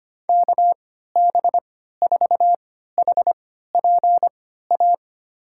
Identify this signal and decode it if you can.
Morse code, heard as a keyed tone, about 25 wpm, spaced as for 14 wpm, 700 Hz. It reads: KB4HPA
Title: USA Call Signs - 6 Characters - 3x Character Spacing - 25wpm